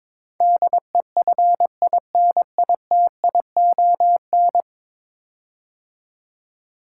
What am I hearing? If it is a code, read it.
Morse code, heard as a keyed tone, about 22 wpm, 700 Hz. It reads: DEFINITION
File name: 10 Letter Words - First 3 Letters Given - 22wpm